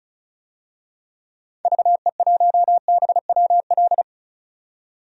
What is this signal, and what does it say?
Morse code, heard as a keyed tone, about 36 wpm, 700 Hz. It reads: VE1BWL
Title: DXCC-100 Call Signs - 6 Characters - 35wpm